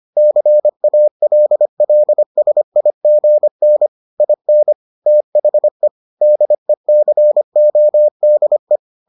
Morse code, heard as a keyed tone, about 25 wpm, 600 Hz. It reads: CALLSIGN IN THE DECODE